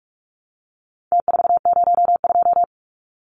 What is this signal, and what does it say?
Morse code, heard as a keyed tone, about 44 wpm, 700 Hz. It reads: T402
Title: Numbers - Contest Style - 4-Digit 45wpm